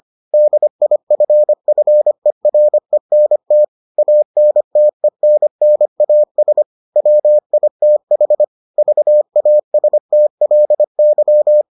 Morse code, heard as a keyed tone, about 25 wpm, 600 Hz. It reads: DIFFERENT ANTENNAS WITH VASTLY